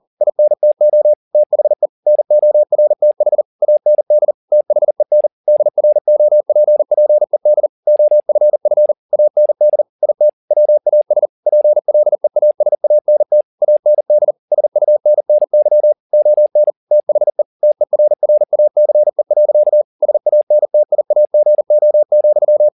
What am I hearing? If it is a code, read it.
Morse code, heard as a keyed tone, about 40 wpm, 600 Hz. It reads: INTO THE NORTH AND THEN DROPPED OFF AND IT WAS PLEASANT AND SUNNY ON THE TERRACE. SANTIAGO,
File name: the old man and the sea 1 40wpm